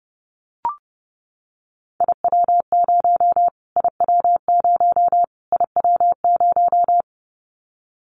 Morse code, heard as a keyed tone, about 30 wpm, 700 Hz. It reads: IW0 IW0 IW0
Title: Callsign Prefixes for Top 25 DXCC - Repeated 3x - 30wpm